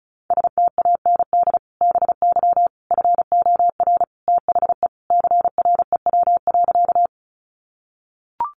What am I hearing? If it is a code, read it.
Morse code, heard as a keyed tone, about 34 wpm, 700 Hz. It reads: STAND BY FOR THE CREW.
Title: Sentences from Top 1500 Words 35wpm